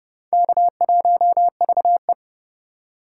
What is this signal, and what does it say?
Morse code, heard as a keyed tone, about 30 wpm, 700 Hz. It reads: K1VE
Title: USA Call Signs - 4 Characters - 2x Character Spacing - 30wpm